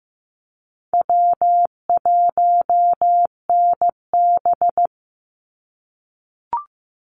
Morse code, heard as a keyed tone, about 15 wpm, 700 Hz. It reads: W1NB
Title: USA Call Signs 15wpm